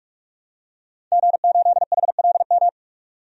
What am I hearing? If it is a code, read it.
Morse code, heard as a keyed tone, about 46 wpm, 700 Hz. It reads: G8HLM
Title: DXCC-100 Call Signs - 5 Characters - 45wpm